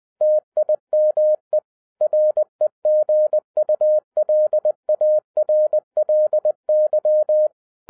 Morse code, heard as a keyed tone, about 20 wpm, 610 Hz. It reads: TIME REGULARLY
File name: 2012-01-mp3 / n16-THROUGH EIGHT MONTHS-20wpm